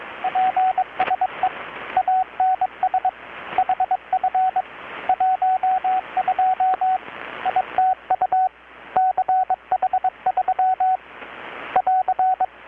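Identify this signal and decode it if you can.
Morse code, heard as a keyed tone, about 22 wpm, 750 Hz, with radio noise. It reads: PSE ANS HF 12 ITU CH3 <AR>
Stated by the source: Morse from KPH's Night of Nights XX maritime radiotelegraphy event, received in Chiapas, Mexico about 3,800 km away on a Kenwood R-600 with a grounded 10 m long-wire antenna.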